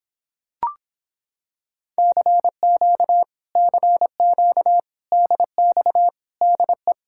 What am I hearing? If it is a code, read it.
Morse code, heard as a keyed tone, about 26 wpm, 700 Hz. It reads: CQ CQ DX DE